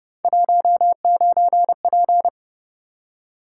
Morse code, heard as a keyed tone, about 29 wpm, 700 Hz. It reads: 19P
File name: Speed-Racing - Three Letter-Number Combinations - 25wpm